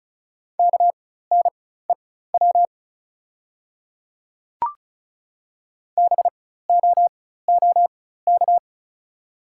Morse code, heard as a keyed tone, about 34 wpm, spaced as for 16 wpm, 700 Hz. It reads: KNEW BOOK
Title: 4x Character Spacing - Top 500 Words - 35wpm